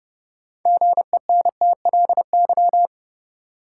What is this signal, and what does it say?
Morse code, heard as a keyed tone, about 31 wpm, 700 Hz. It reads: GENTLY
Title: Sentences from Top 1500 Words - New Word Drill - 30wpm